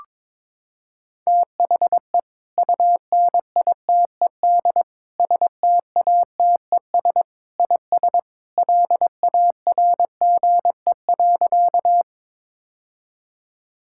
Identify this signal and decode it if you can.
Morse code, heard as a keyed tone, about 22 wpm, 700 Hz. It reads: THE UNITED STATES IS LARGE.
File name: practice / 22wpm / Sentences with 3-Word Phrases - 22wpm